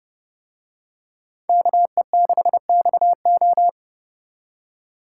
Morse code, heard as a keyed tone, about 30 wpm, 700 Hz. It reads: KE6XO